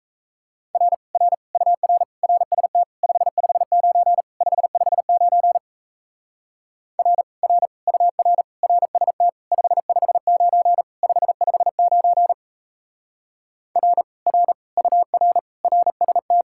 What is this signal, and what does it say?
Morse code, heard as a keyed tone, about 36 wpm, 700 Hz. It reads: R R UR RST 559 559 R R UR RST 559 559 R R UR RST